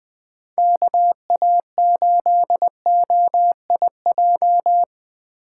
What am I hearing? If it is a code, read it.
Morse code, heard as a keyed tone, about 20 wpm, 700 Hz. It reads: KA8OIJ
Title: Speed Racing - USA 2x3 Call Signs 17wpm